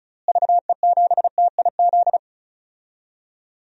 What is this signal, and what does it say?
Morse code, heard as a keyed tone, about 34 wpm, 700 Hz. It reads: VE7TIZ
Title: DXCC-100 Call Signs - 6 Characters - Repeated 2x - 35wpm